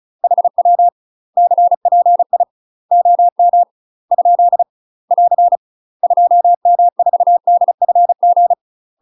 Morse code, heard as a keyed tone, about 34 wpm, 700 Hz. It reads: HW CPI OM ? <AR> 2M4DFG
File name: UK MASTER QSOs TRAINING / 35WPM / UK MASTER QSOs 0005